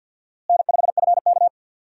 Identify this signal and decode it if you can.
Morse code, heard as a keyed tone, about 53 wpm, 700 Hz. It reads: N5FK
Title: USA Call Signs - 4 Characters - Repeated 2x - 50wpm